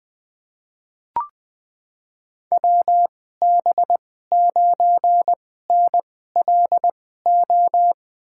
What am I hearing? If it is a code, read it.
Morse code, heard as a keyed tone, about 20 wpm, spaced as for 14 wpm, 700 Hz. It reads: WB9NLO
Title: USA Call Signs - 6 Characters - 2x Character Spacing - 20wpm